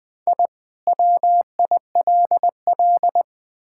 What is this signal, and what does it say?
Morse code, heard as a keyed tone, about 21 wpm, 700 Hz. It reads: I WILL
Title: Sentences from Top 1000 Words - Review Part 5 - 20wpm